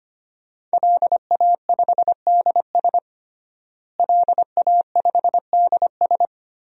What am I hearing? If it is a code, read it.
Morse code, heard as a keyed tone, about 25 wpm, 700 Hz. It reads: LA5DS LA5DS
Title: DXCC-100 Call Signs - 5 Characters - Repeated 3x - 25wpm